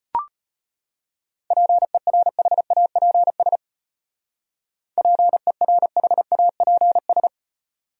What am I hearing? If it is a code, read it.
Morse code, heard as a keyed tone, about 31 wpm, 700 Hz. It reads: PERHAPS PERHAPS
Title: Speed-Racing Top 500 Words 25wpm